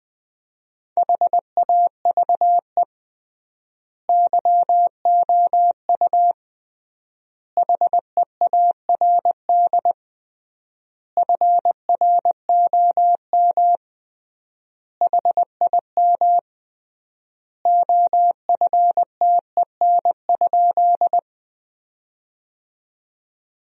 Morse code, heard as a keyed tone, about 21 wpm, 700 Hz. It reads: HAVE YOU HEARD FROM HIM OFTEN?
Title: 3x Word-Spacing - Sentences from Top 400 Words - 20wpm